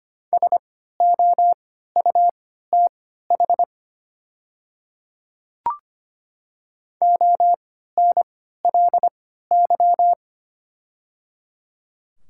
Morse code, heard as a keyed tone, about 25 wpm, spaced as for 14 wpm, 690 Hz. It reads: SOUTH ONLY
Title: 3x Character Spacing - Top 500 Words - 25wpm